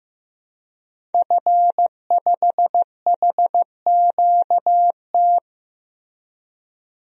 Morse code, heard as a keyed tone, about 15 wpm, 700 Hz. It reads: F5HQT
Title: DXCC-100 Call Signs - 5 Characters - Repeated 2x - 15wpm